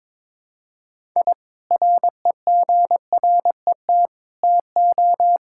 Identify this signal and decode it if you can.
Morse code, heard as a keyed tone, about 22 wpm, 700 Hz. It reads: I REGRET TO
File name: practice / 22wpm / Sentences with 3-Word Phrases - 22wpm